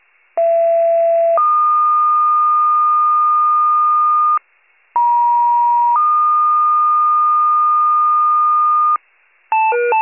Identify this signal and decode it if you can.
Two-tone page, heard as a one-second tone, then a three-second tone, about 670 then 1220 Hz, about 950 then 1220 Hz. It is heard from 0.4 to 9.0 s.
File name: pager / 83 RESCUE OFC / wcalert